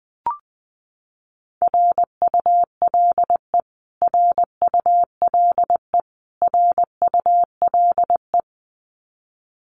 Morse code, heard as a keyed tone, about 19 wpm, 700 Hz. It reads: RULE RULE RULE
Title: Top 500 Words Repeated 20wpm